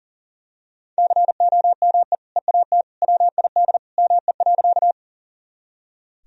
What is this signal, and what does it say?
Morse code, heard as a keyed tone, about 37 wpm, 700 Hz. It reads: COME EAT WID ME.
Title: Sentences from Top 300 Words and Abbreviations 40wpm